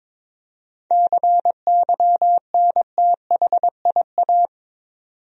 Morse code, heard as a keyed tone, about 23 wpm, 700 Hz. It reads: CYNTHIA
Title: USA Common Names - 22wpm